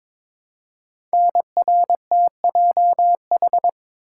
Morse code, heard as a keyed tone, about 22 wpm, 700 Hz. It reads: NRTJH
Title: Code Groups - 5 Letters - 22wpm